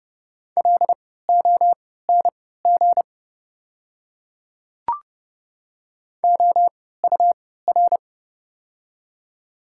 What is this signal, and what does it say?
Morse code, heard as a keyed tone, about 29 wpm, spaced as for 17 wpm, 690 Hz. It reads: LONG OUR
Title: 3x Character Spacing - Top 200 Words - 30wpm